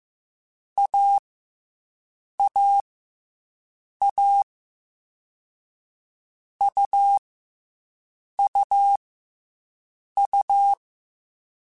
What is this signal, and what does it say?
Morse code, heard as a keyed tone, about 15 wpm, spaced as for 6 wpm, 800 Hz. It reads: AAA UUU
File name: Difficult Combos / Difficult Combos-mp3s / AUV ShortRun @13